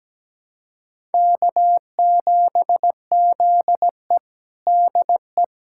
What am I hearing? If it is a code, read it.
Morse code, heard as a keyed tone, about 17 wpm, 700 Hz. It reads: K7ZE DE
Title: USA Callsign DE Callsign - 4 Characters - 17wpm